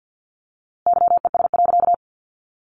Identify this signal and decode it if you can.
Morse code, heard as a keyed tone, about 50 wpm, 700 Hz. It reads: YES.
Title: A Sound Of Thunder 50wpm